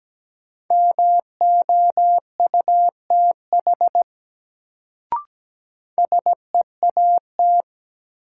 Morse code, heard as a keyed tone, about 17 wpm, 700 Hz. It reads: MOUTH SEAT
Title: Sentences from Top 1500 Words - New Word Drill - 17wpm